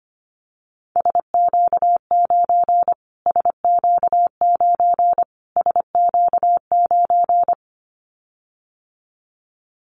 Morse code, heard as a keyed tone, about 25 wpm, 700 Hz. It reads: SQ9 SQ9 SQ9
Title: Callsign Prefixes for Top 25 DXCC - Repeated 3x - 25wpm